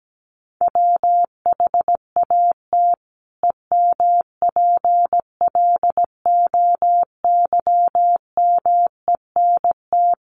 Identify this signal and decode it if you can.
Morse code, heard as a keyed tone, about 17 wpm, 700 Hz. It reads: WHAT EMPLOYMENT
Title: Sentences from Top 1500 Words 17wpm